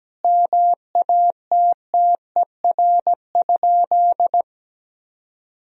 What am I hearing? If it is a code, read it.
Morse code, heard as a keyed tone, about 17 wpm, 700 Hz. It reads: MATTER?
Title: Sentences from Top 1000 Words - Review Part 4 - 17wpm